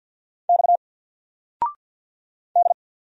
Morse code, heard as a keyed tone, about 50 wpm, 700 Hz. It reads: X D
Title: Single Letter-Number 50wpm